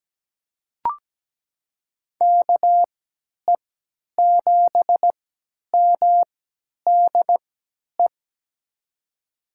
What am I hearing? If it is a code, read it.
Morse code, heard as a keyed tone, about 17 wpm, spaced as for 10 wpm, 700 Hz. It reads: KE7MDE
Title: USA Call Signs - 6 Characters - 3x Character Spacing - 17wpm